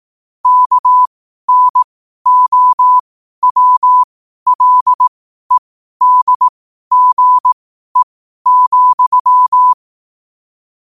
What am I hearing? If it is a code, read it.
Morse code, heard as a keyed tone, about 18 wpm, spaced as for 12 wpm, 1000 Hz. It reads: KNOWLEDGE,
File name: qotdfiles / 12wpm